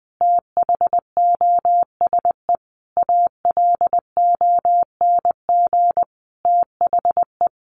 Morse code, heard as a keyed tone, about 20 wpm, 700 Hz. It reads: THOSE ALONG THE
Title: Sentences from Top 1500 Words 20wpm